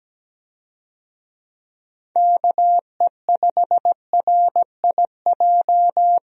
Morse code, heard as a keyed tone, about 17 wpm, 700 Hz. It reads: KE5RIJ
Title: USA Call Signs - 6 Characters - 17wpm